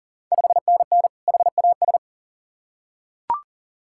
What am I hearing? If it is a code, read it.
Morse code, heard as a keyed tone, about 37 wpm, 700 Hz. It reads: 5NN HAS
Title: Ontario QSO Party Exchange 40wpm